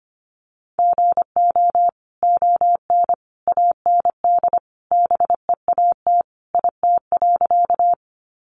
Morse code, heard as a keyed tone, about 25 wpm, 700 Hz. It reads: GO ON AND BEAT IT.
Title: Sentences from Top 1300 Words 25wpm